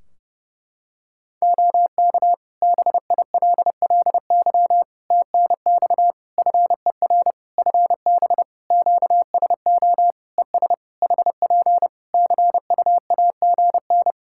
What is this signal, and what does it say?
Morse code, heard as a keyed tone, about 30 wpm, 700 Hz. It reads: OK BILLY TNX FER FB QSO ES HP CUAGN
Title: Practice QSO - Exchange 4 - Ending - 30wpm